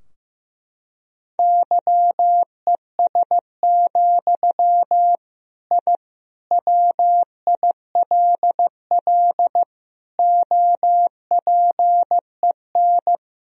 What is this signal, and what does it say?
Morse code, heard as a keyed tone, about 15 wpm, 700 Hz. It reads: YES, I WILL OPEN
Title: Sentences from Top 1000 Words - Review Part 2 - 15wpm